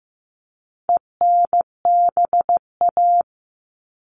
Morse code, heard as a keyed tone, about 15 wpm, 700 Hz. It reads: ENBA
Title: All Cut Numbers - 4-Digit 15wpm